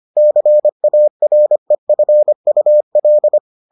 Morse code, heard as a keyed tone, about 25 wpm, 600 Hz. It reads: CAREFUL